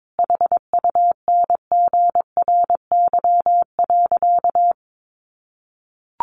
Morse code, heard as a keyed tone, about 22 wpm, 700 Hz. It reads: HUNGRY.
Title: Sentences from Top 1400 Words 22wpm